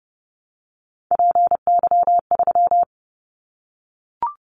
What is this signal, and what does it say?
Morse code, heard as a keyed tone, about 31 wpm, 700 Hz. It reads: PY3
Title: Callsign Prefixes for Top 50 DXCC - 30wpm